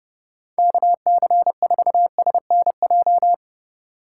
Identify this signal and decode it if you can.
Morse code, heard as a keyed tone, about 30 wpm, 700 Hz. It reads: KC4SNJ
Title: USA Call Signs - 6 Characters - 3x Character Spacing - 30wpm